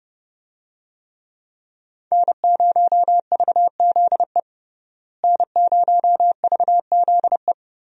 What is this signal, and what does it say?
Morse code, heard as a keyed tone, about 30 wpm, 700 Hz. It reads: N0VZE N0VZE
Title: USA Call Signs - 5 Characters - Repeated 3x - 30wpm